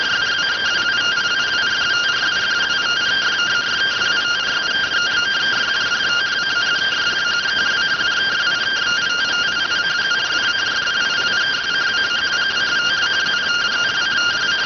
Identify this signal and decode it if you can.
Two-tone teleprinter signal, about 170 Hz shift.